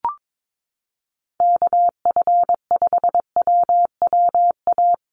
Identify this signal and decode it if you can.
Morse code, heard as a keyed tone, about 22 wpm, 700 Hz. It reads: KF5WWA
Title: USA Call Signs Repeated 3x 22wpm